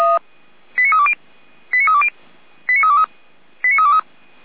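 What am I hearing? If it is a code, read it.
Five-tone selective call (ZVEI), heard as quick runs of five tones, tones of about 70 ms. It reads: 89320 89320 89323 89323
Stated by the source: ZVEI five-tone call 89067 received on SDR, from the Feuerwehr Orlamünde (fire brigade) audio server.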